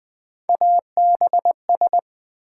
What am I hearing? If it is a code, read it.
Morse code, heard as a keyed tone, about 20 wpm, 700 Hz. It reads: ABS
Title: Build A Word - 3 Letters At A Time - 6-Letter Words - 20wpm